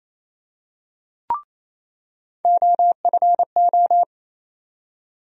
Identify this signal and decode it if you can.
Morse code, heard as a keyed tone, about 28 wpm, 700 Hz. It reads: OFO
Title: USA Call Sign Postfixes - 3 Letters 28wpm